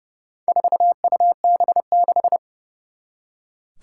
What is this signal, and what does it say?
Morse code, heard as a keyed tone, about 32 wpm, 700 Hz. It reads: 4UB6
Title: Four Challenging Letter-Number Combinations 30wpm